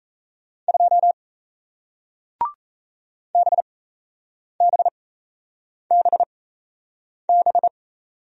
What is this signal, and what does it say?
Morse code, heard as a keyed tone, about 38 wpm, spaced as for 8 wpm, 700 Hz. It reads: 2 BBBB